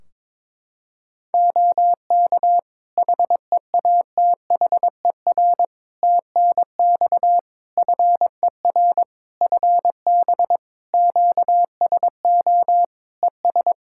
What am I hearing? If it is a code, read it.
Morse code, heard as a keyed tone, about 22 wpm, 700 Hz. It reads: OK HEATHER TNX FER FB QSO ES
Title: K1USN SST QSO Practice 22wpm